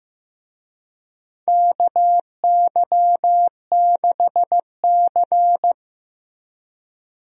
Morse code, heard as a keyed tone, about 15 wpm, 700 Hz. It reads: KY6C
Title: USA Call Signs - 4 Characters - Repeated 2x - 15wpm